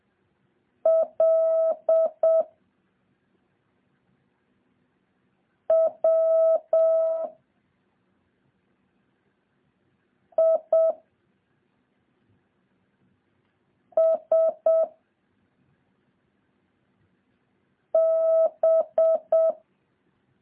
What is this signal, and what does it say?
Morse code, heard as a keyed tone, about 7 wpm, 650 Hz. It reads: L W I S B
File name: jh8cbh / 2023cw